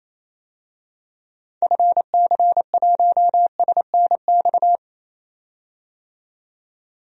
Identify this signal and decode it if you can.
Morse code, heard as a keyed tone, about 27 wpm, 700 Hz. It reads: FC1SNX
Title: DXCC-100 Call Signs - 6 Characters - 28wpm